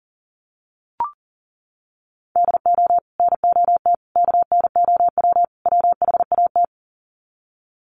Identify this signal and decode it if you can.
Morse code, heard as a keyed tone, about 40 wpm, 700 Hz. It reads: DO NOT KNOW WHAT
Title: Sets of 4 Words 40wpm